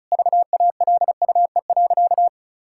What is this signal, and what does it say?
Morse code, heard as a keyed tone, about 36 wpm, 700 Hz. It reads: VALUE.
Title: Sentences from Top 1000 Words - Review Part 6 - 35wpm